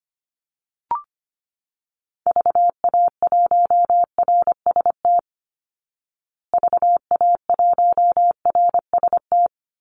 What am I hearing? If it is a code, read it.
Morse code, heard as a keyed tone, about 25 wpm, 700 Hz. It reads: VA1RST VA1RST
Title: Call Signs Repeated 3x - 25wpm